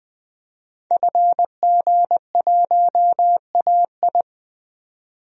Morse code, heard as a keyed tone, about 20 wpm, 700 Hz. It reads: FG1AI
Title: DXCC-100 Call Signs - 5 Characters - 20wpm